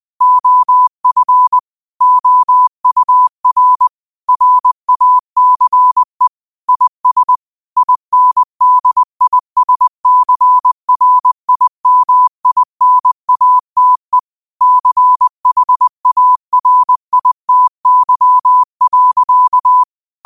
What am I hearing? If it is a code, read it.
Morse code, heard as a keyed tone, about 20 wpm, 1000 Hz. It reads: OF OUR RACE IS INDISCRIMINATE CHARITY.